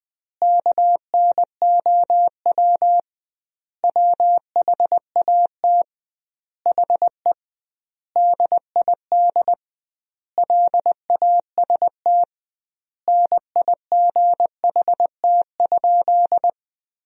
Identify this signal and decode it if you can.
Morse code, heard as a keyed tone, about 20 wpm, 700 Hz. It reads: KNOW WHAT HE DID LAST NIGHT?
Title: 2x Word-Spacing - Sentences from Top 300 Words - 20wpm